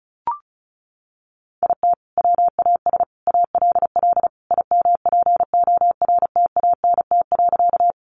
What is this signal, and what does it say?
Morse code, heard as a keyed tone, about 34 wpm, 700 Hz. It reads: IT WAS ALL IMPORTANT.
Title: Sentences from Top 600 Words 35wpm